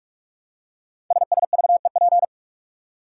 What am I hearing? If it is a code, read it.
Morse code, heard as a keyed tone, about 44 wpm, 700 Hz. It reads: SSVEP
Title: Code Groups - 5 Letters - 45wpm